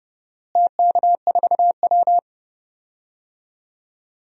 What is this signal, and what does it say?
Morse code, heard as a keyed tone, about 30 wpm, 700 Hz. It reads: TK4W
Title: DXCC-100 Call Signs - 4 Characters - Repeated 2x - 30wpm